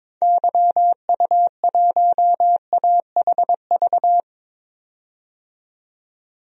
Morse code, heard as a keyed tone, about 22 wpm, 700 Hz. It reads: YU1AHV
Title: DXCC-100 Call Signs - 6 Characters - 22wpm